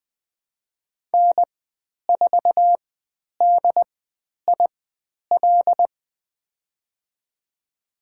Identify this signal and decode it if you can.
Morse code, heard as a keyed tone, about 20 wpm, spaced as for 10 wpm, 700 Hz. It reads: N4DIL
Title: 102 Lesson 17g - Letter u - Call Signs 20wpm